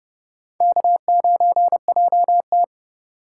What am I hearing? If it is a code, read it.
Morse code, heard as a keyed tone, about 36 wpm, 700 Hz. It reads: K9JT
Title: Speed Racing - USA Call Signs 30wpm